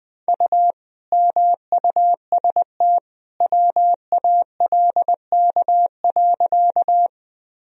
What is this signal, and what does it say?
Morse code, heard as a keyed tone, about 20 wpm, 700 Hz. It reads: U MUST WALK.
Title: Sentences from Top 300 Words and Abbreviations 20wpm